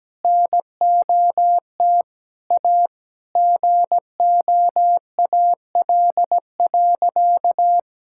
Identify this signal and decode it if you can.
Morse code, heard as a keyed tone, about 17 wpm, 700 Hz. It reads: NOT A GOAL.
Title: Sentences from Top 1000 Words - Review Part 5 - 17wpm